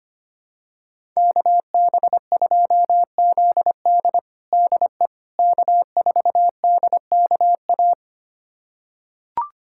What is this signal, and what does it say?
Morse code, heard as a keyed tone, about 25 wpm, 700 Hz. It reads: KB2ZD DE K4DKA
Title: USA Callsign DE Callsign - 5 Characters - 25wpm